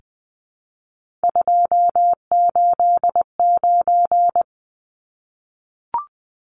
Morse code, heard as a keyed tone, about 20 wpm, 700 Hz. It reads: 289
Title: Numbers - 3-Digit 20wpm